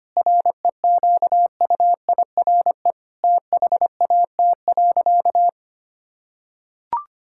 Morse code, heard as a keyed tone, about 25 wpm, 700 Hz. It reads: REQUIRE THAT.
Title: Sentences from Top 1000 Words - Review Part 6 - 25wpm